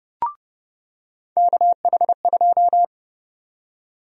Morse code, heard as a keyed tone, about 30 wpm, 700 Hz. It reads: KH2
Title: USA Call Sign Prefixes - 30wpm